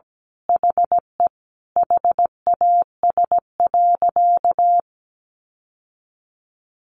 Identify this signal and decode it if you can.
Morse code, heard as a keyed tone, about 17 wpm, 700 Hz. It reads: HE HAS.
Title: Sentences from Top 1100 Words 17wpm